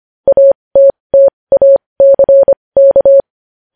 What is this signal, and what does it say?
Morse code, heard as a keyed tone, about 25 wpm, spaced as for 20 wpm, 550 Hz. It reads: ATTACK
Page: news-headlines-in-morse-code-at-20-wpm